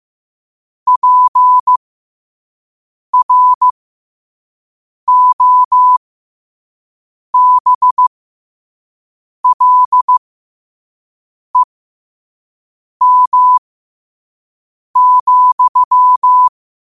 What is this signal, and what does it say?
Morse code, heard as a keyed tone, about 15 wpm, spaced as for 5 wpm, 1000 Hz. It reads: PROBLEM,
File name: 5wpm